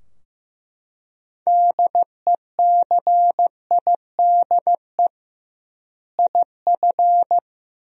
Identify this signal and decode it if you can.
Morse code, heard as a keyed tone, about 15 wpm, 700 Hz. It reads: DECIDE IF
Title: 2x Word-Spacing - Sentences from Top 500 Words - 15wpm